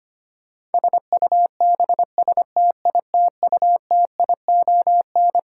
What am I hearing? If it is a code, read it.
Morse code, heard as a keyed tone, about 25 wpm, 700 Hz. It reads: SUBSTITUTION
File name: Twelve Letter Words - 25wpm